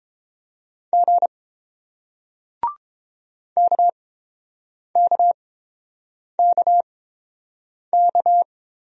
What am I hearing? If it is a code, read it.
Morse code, heard as a keyed tone, about 30 wpm, spaced as for 7 wpm, 700 Hz. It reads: G KKKK